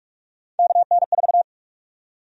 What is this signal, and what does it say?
Morse code, heard as a keyed tone, about 42 wpm, 700 Hz. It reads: KN4
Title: USA Call Sign Prefixes - Repeated 2x - 45wpm